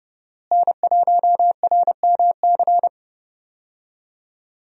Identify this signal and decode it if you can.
Morse code, heard as a keyed tone, about 30 wpm, 700 Hz. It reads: N1RMC
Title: USA Call Signs - 5 Characters - Repeated 3x - 30wpm